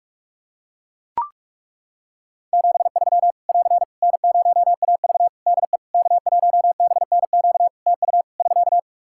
Morse code, heard as a keyed tone, about 42 wpm, 700 Hz. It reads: 73 <AR> N0AV DE K1BNQ TU <SK>
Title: Practice QSO - Exchange 4 - Ending - 45wpm